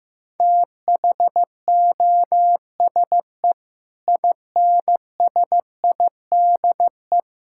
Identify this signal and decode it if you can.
Morse code, heard as a keyed tone, about 15 wpm, 700 Hz. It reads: THOSE INSIDE
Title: Sentences from Top 1000 Words - Review Part 6 - 15wpm